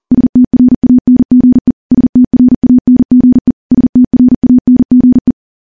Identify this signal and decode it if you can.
Morse code, heard as a keyed tone, about 42 wpm, 250 Hz. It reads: STRANGE STRANGE STRANGE
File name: morse / morse-training-40WPM